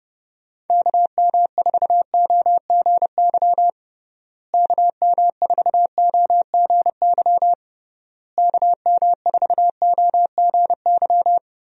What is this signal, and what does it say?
Morse code, heard as a keyed tone, about 32 wpm, 700 Hz. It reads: KM4OGY KM4OGY KM4OGY
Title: USA Call Signs - 6 Characters - Repeated 3x - 30wpm